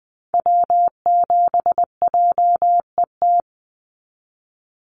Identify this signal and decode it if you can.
Morse code, heard as a keyed tone, about 20 wpm, 700 Hz. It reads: W7JET
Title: NA SOTA Activator Callsigns 20wpm